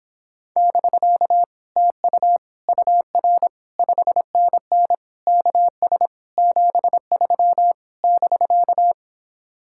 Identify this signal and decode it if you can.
Morse code, heard as a keyed tone, about 26 wpm, 700 Hz. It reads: <BK> TU UR 5NN KS 73 <BK>